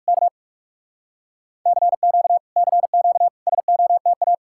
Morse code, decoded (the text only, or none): K CQ CQ SOTA